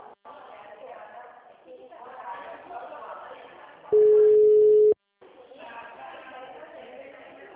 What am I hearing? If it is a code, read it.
Ringback tone (Europe, 425 Hz), with speech.